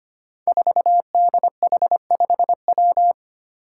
Morse code, heard as a keyed tone, about 25 wpm, 700 Hz. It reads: 4DH5W
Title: Five Challenging Letter-Number Combinations 25wpm